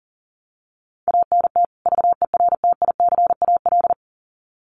Morse code, heard as a keyed tone, about 40 wpm, 700 Hz. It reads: ANT VERTICAL
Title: Common QSO Phrases 40wpm